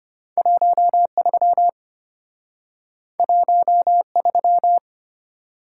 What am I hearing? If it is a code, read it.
Morse code, heard as a keyed tone, about 34 wpm, 700 Hz. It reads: 13 13